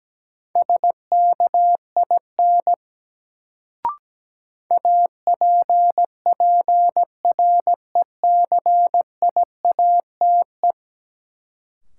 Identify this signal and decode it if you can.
Morse code, heard as a keyed tone, about 17 wpm, 700 Hz. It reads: SKIN APPRECIATE
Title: Sentences from Top 1300 Words - New Word Drill - 17wpm